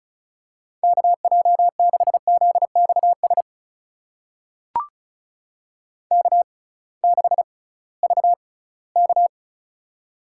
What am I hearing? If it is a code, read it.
Morse code, heard as a keyed tone, about 36 wpm, spaced as for 20 wpm, 700 Hz. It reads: KJ6ZXS K6VK
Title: USA Call Signs - 6x Character Spacing - 35wpm